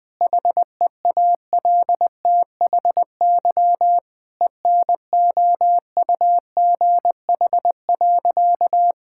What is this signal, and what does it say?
Morse code, heard as a keyed tone, about 21 wpm, 700 Hz. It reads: HEALTHY ENOUGH.
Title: Sentences from Top 1000 Words - Review Part 5 - 20wpm